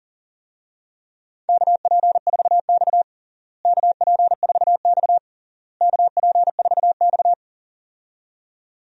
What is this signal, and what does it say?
Morse code, heard as a keyed tone, about 40 wpm, 700 Hz. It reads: KP4X KP4X KP4X
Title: DXCC-100 Call Signs - 4 Characters - Repeated 3x - 40wpm